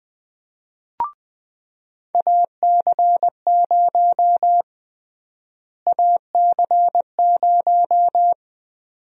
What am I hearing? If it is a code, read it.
Morse code, heard as a keyed tone, about 20 wpm, 700 Hz. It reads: AC0 AC0
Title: USA Call Sign Prefixes - Repeated 2x - 20wpm